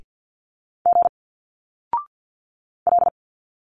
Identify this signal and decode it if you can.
Morse code, heard as a keyed tone, about 50 wpm, 700 Hz. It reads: G L
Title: Single Letters 50wpm